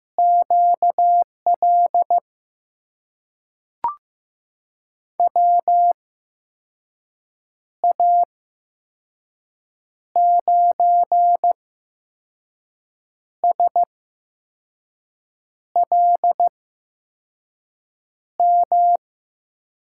Morse code, heard as a keyed tone, about 15 wpm, spaced as for 5 wpm, 700 Hz. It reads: QL WA9SLM
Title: USA Call Signs - 6 Characters - 8x Character Spacing - 15wpm